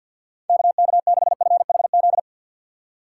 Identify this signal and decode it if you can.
Morse code, heard as a keyed tone, about 65 wpm, spaced as for 57 wpm, 700 Hz. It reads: KK6FHZ
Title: Speed Racing - USA 2x3 Call Signs 50wpm